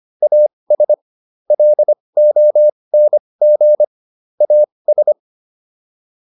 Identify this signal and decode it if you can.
Morse code, heard as a keyed tone, about 25 wpm, spaced as for 20 wpm, 600 Hz. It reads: AS LONG AS